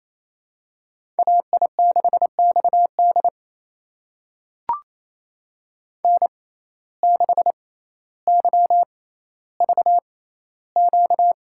Morse code, heard as a keyed tone, about 28 wpm, spaced as for 14 wpm, 700 Hz. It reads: AI6XD N6YVQ
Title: USA Call Signs - 5 Characters - 6x Character Spacing - 28wpm